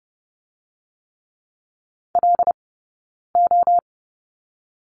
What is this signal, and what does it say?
Morse code, heard as a keyed tone, about 30 wpm, 700 Hz. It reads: L O
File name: Lesson 19d - Letter w - Two Characters 30wpm